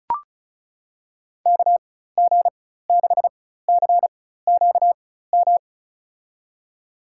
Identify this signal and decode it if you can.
Morse code, heard as a keyed tone, about 36 wpm, spaced as for 16 wpm, 700 Hz. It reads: KG6CQM
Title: USA Call Signs - 6 Characters - 4x Character Spacing - 35wpm